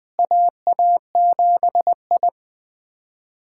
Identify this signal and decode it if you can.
Morse code, heard as a keyed tone, about 21 wpm, 700 Hz. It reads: AA7I
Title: USA Callsign DE Callsign - 4 Characters - 20wpm